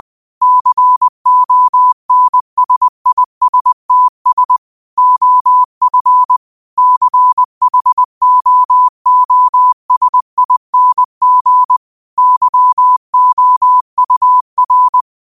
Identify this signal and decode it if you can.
Morse code, heard as a keyed tone, about 20 wpm, 1000 Hz. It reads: CONSISTS OF CHOOSING YOUR
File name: qotdfiles / 20wpm